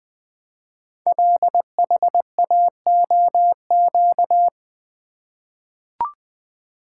Morse code, heard as a keyed tone, about 20 wpm, 700 Hz. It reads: LHAOQ
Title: Code Groups - 5 Letters - 20wpm